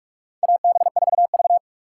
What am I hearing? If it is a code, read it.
Morse code, heard as a keyed tone, about 44 wpm, 700 Hz. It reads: AB4V
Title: USA Call Signs - 2x Character Spacing - 45wpm